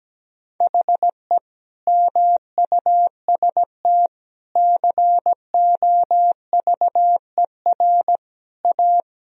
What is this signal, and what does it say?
Morse code, heard as a keyed tone, about 17 wpm, 700 Hz. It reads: HE MUST COVER A
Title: Sentences from Top 500 Words and Abbreviations 17wpm